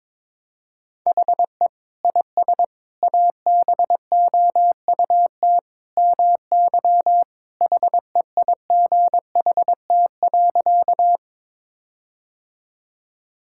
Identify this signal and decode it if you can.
Morse code, heard as a keyed tone, about 22 wpm, 700 Hz. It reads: HE IS ABOUT MY HEIGHT.
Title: Sentences from Top 1600 Words 22wpm